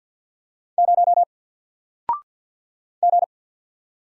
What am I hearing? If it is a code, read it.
Morse code, heard as a keyed tone, about 54 wpm, 700 Hz. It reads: 0 G
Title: Single Letter-Number Alt 50wpm